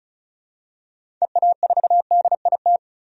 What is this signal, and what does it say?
Morse code, heard as a keyed tone, about 34 wpm, 700 Hz. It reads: EA4DIT
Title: DXCC-100 Call Signs - 6 Characters - 35wpm